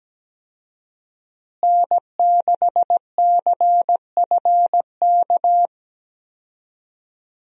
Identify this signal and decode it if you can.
Morse code, heard as a keyed tone, about 17 wpm, 700 Hz. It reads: N6CFK
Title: USA Call Signs - 5 Characters - Repeated 3x - 17wpm